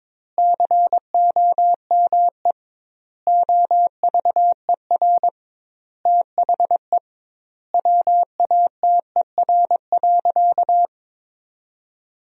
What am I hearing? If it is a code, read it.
Morse code, heard as a keyed tone, about 22 wpm, 700 Hz. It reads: COME OVER THE WATER.
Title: 2x Word-Spacing - Sentences from Top 300 Words - 22wpm